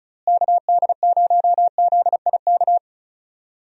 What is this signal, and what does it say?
Morse code, heard as a keyed tone, about 36 wpm, 700 Hz. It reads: KD0ZIK
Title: USA Call Signs - 6 Characters - Repeated 3x - 35wpm